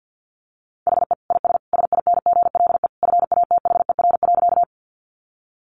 Morse code, heard as a keyed tone, about 50 wpm, 700 Hz. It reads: HE IS SINGLE FATHER.
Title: Sentences from Top 700 Words 50wpm